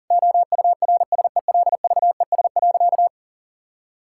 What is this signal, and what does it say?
Morse code, heard as a keyed tone, about 40 wpm, 700 Hz. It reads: OURSELVES.
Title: Sentences from Top 1600 Words 40wpm